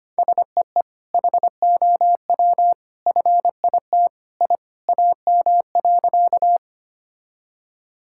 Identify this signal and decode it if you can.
Morse code, heard as a keyed tone, about 25 wpm, 700 Hz. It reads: SEE HOW FIT I AM.
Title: Sentences from Top 1000 Words - Review Part 5 - 25wpm